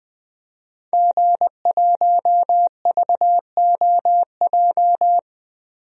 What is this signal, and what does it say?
Morse code, heard as a keyed tone, about 21 wpm, 700 Hz. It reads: G1VOJ
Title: DXCC-100 Call Signs - 5 Characters - Repeated 3x - 20wpm